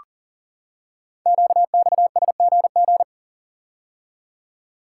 Morse code, heard as a keyed tone, about 38 wpm, 690 Hz. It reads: QXSGG
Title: Code Groups - 5 Letter:Numbers - 40wpm